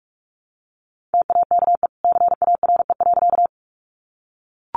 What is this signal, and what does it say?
Morse code, heard as a keyed tone, about 44 wpm, 700 Hz. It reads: TAKE CARE.
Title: Sentences from Top 300 Words 45wpm